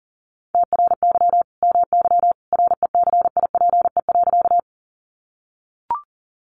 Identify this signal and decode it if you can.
Morse code, heard as a keyed tone, about 40 wpm, 700 Hz. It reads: TRY MY RECIPE.
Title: Sentences from Top 1200 Words 40wpm